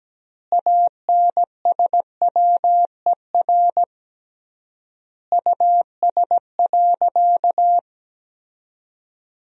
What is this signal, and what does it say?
Morse code, heard as a keyed tone, about 17 wpm, 700 Hz. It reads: ANSWER US.
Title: 3x Word-Spacing - Sentences from Top 300 Words - 17wpm